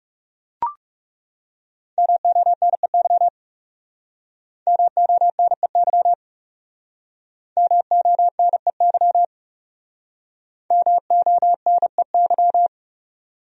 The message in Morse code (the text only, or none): MONEY MONEY MONEY MONEY